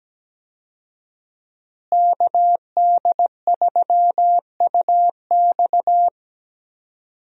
Morse code, heard as a keyed tone, about 17 wpm, 700 Hz. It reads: KD3UX
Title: USA Call Signs - 5 Characters - Repeated 3x - 17wpm